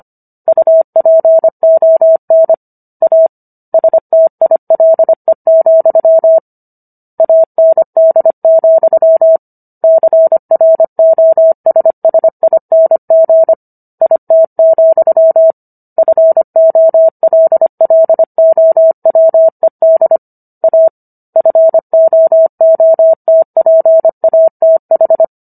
Morse code, heard as a keyed tone, about 25 wpm, 650 Hz. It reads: UPON A STILE, AND, CROSSING IT, FOLLOWED A FOOTPATH